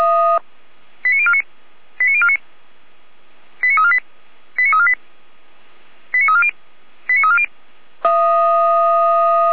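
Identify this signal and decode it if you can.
Five-tone selective call (ZVEI), heard as quick runs of five tones, tones of about 70 ms. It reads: 89049 89049 89348 89348 89340 89340
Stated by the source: ZVEI five-tone selective call 89045 received by SDR.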